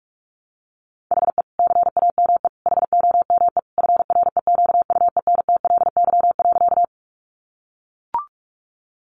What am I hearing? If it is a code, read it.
Morse code, heard as a keyed tone, about 44 wpm, 700 Hz. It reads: HE CAME HOME FREQUENTLY.
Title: Sentences from Top 1300 Words 45wpm